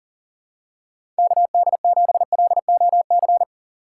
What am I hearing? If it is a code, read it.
Morse code, heard as a keyed tone, about 40 wpm, 700 Hz. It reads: KD7LOC
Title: USA Call Signs - 6 Characters - 2x Character Spacing - 40wpm